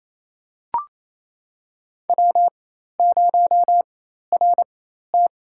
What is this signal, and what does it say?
Morse code, heard as a keyed tone, about 28 wpm, spaced as for 13 wpm, 700 Hz. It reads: W0RT
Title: USA Call Signs - 4x Character Spacing - 28wpm